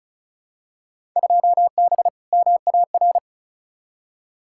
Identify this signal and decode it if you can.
Morse code, heard as a keyed tone, about 34 wpm, 700 Hz. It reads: 2B MAR
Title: Field Day Exchanges Encore - 35wpm